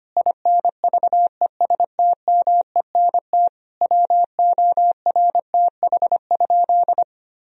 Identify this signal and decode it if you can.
Morse code, heard as a keyed tone, about 25 wpm, 700 Hz. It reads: INVESTMENT WORTH?
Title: Sentences from Top 1000 Words - Review Part 4 - 25wpm